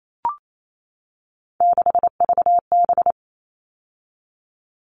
Morse code, heard as a keyed tone, about 28 wpm, 700 Hz. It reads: 6VB
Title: All Cut Numbers - 3-Digit 28wpm